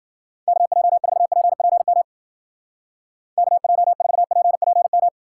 Morse code, heard as a keyed tone, about 51 wpm, 700 Hz. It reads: KJ4PPM KJ4PPM